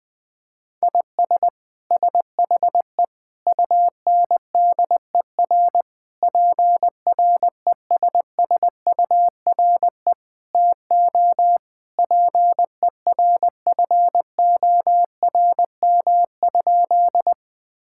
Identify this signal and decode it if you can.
Morse code, heard as a keyed tone, about 20 wpm, 700 Hz. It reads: IS SHE UNDER PRESSURE TO PERFORM?
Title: Sentences from Top 1000 Words - Review Part 6 - 20wpm